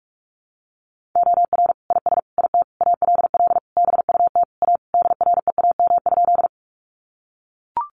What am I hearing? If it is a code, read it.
Morse code, heard as a keyed tone, about 44 wpm, 700 Hz. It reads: OR IS IT ALL BUT A DREAM?
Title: Sentences from Top 1500 Words 45wpm